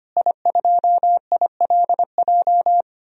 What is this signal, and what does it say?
Morse code, heard as a keyed tone, about 25 wpm, 700 Hz. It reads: I2ILJ
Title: DXCC-100 Call Signs - 5 Characters - Repeated 2x - 25wpm